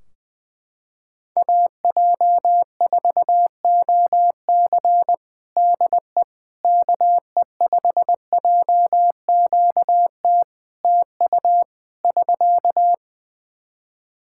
Morse code, heard as a keyed tone, about 20 wpm, 700 Hz. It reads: AJ4OC DE KE5JQT TU <SK>
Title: K1USN SST QSO Practice 20wpm